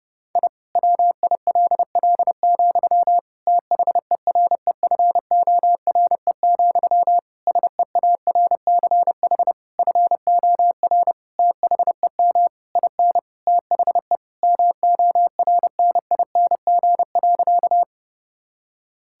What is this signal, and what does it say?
Morse code, heard as a keyed tone, about 30 wpm, 700 Hz. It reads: I WILL, THEREFORE, SEARCH FOR THEM IN THE MORNING.
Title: Sentences from Top 1000 Words - Review Part 5 - 30wpm